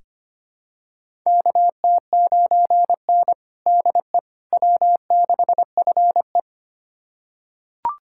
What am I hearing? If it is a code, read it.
Morse code, heard as a keyed tone, about 25 wpm, 700 Hz. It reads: KT9N DE W6FE
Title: USA Callsign DE Callsign - 4 Characters - 25wpm